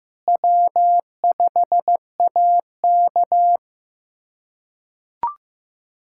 Morse code, heard as a keyed tone, about 15 wpm, 700 Hz. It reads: W5AK
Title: USA Callsign DE Callsign 15wpm